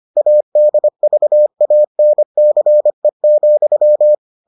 Morse code, heard as a keyed tone, about 25 wpm, 600 Hz. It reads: ADVANCE,